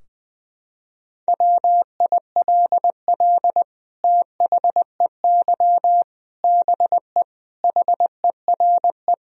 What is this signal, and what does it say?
Morse code, heard as a keyed tone, about 20 wpm, 700 Hz. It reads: WILL THEY BE HERE
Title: Sentences from Top 1000 Words - Review Part 4 - 20wpm